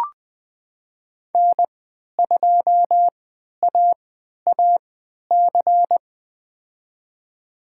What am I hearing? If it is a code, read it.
Morse code, heard as a keyed tone, about 20 wpm, spaced as for 11 wpm, 700 Hz. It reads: N2AAC
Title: USA Call Signs - 5 Characters - 3x Character Spacing - 20wpm